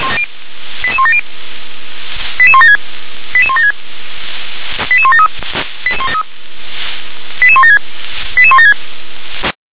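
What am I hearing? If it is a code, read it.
Five-tone selective call (ZVEI), heard as quick runs of five tones, tones of about 70 ms. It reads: 88170 88176 88176 88173 88176 88176
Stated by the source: ZVEI five-tone selective call received by SDR.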